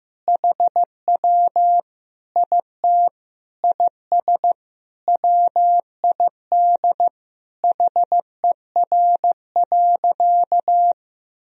Morse code, heard as a keyed tone, about 15 wpm, 700 Hz. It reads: HW IT IS WID HER.
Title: Sentences from Top 100 Words and Abbreviations 15wpm